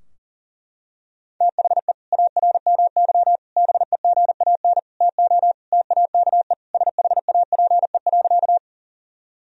Morse code, heard as a keyed tone, about 40 wpm, 700 Hz. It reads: THE ARMY BEGAN TO TAKE SHAPE.
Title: Sentences from Top 1000 Words - Review Part 3 - 40wpm